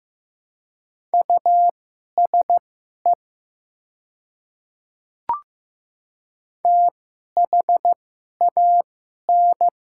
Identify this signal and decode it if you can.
Morse code, heard as a keyed tone, about 15 wpm, spaced as for 11 wpm, 700 Hz. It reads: USE THAN
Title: 2x Character Spacing - Top 500 Words - 15wpm